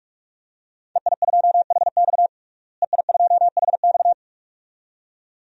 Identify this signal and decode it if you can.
Morse code, heard as a keyed tone, about 46 wpm, 700 Hz. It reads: EI2HX EI2HX
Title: DXCC-100 Call Signs - 5 Characters - Repeated 2x - 45wpm